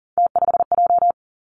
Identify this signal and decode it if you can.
Morse code, heard as a keyed tone, about 42 wpm, 690 Hz. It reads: T5J
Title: Three Challenging Letter-Number Combinations 40wpm